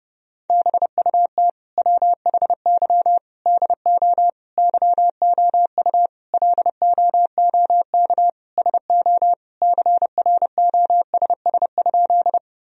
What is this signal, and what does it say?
Morse code, heard as a keyed tone, about 30 wpm, 700 Hz. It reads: BUT WHY DO YOU LOOK SO CROSS?
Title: Sentences from Top 1000 Words - Review Part 1 - 30wpm